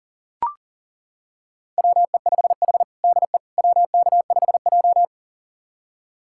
Morse code, heard as a keyed tone, about 37 wpm, 700 Hz. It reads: WE5H DE WK5J
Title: USA Callsign DE Callsign - 4 Characters - 40wpm